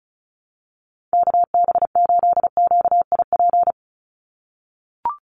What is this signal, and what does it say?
Morse code, heard as a keyed tone, about 34 wpm, 700 Hz. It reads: KB8QIP
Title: USA Call Signs 35wpm